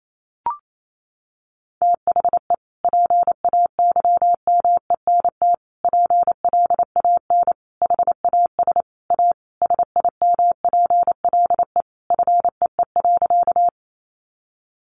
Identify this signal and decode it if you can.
Morse code, heard as a keyed tone, about 28 wpm, 700 Hz. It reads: THE PAYMENT PLAN HAS A SIMPLE FEE.
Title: Sentences from Top 1500 Words 28wpm